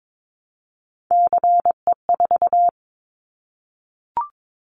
Morse code, heard as a keyed tone, about 22 wpm, 700 Hz. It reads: CE4
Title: Callsign Prefixes for Top 100 DXCC - 22wpm